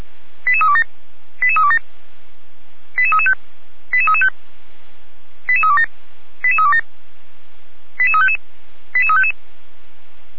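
Five-tone selective call (ZVEI), heard as quick runs of five tones, tones of about 70 ms. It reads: 88327 88327 88335 88335 88327 88327 88355 88355
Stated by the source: Five-tone ZVEI selective call, received with an SDR.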